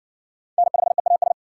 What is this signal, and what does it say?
Morse code, heard as a keyed tone, about 54 wpm, 700 Hz. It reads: N5EAS